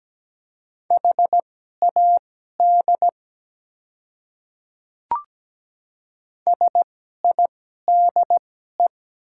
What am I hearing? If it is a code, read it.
Morse code, heard as a keyed tone, about 17 wpm, spaced as for 12 wpm, 690 Hz. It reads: HAD SIDE
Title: 2x Character Spacing - Top 100 Words - 17wpm